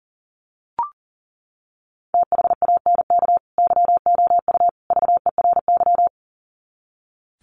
Morse code, heard as a keyed tone, about 40 wpm, 700 Hz. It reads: THANK YOU VERY
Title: Sets of 3 Words 40wpm